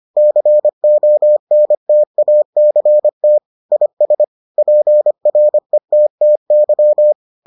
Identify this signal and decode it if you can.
Morse code, heard as a keyed tone, about 25 wpm, 600 Hz. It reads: CONTACT IS PRETTY